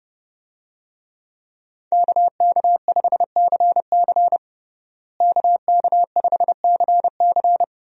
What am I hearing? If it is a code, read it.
Morse code, heard as a keyed tone, about 32 wpm, 700 Hz. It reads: KK5CC KK5CC
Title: USA Call Signs - 5 Characters - Repeated 3x - 30wpm